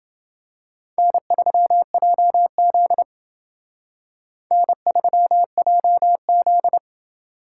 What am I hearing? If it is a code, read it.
Morse code, heard as a keyed tone, about 29 wpm, 700 Hz. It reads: N3JZ N3JZ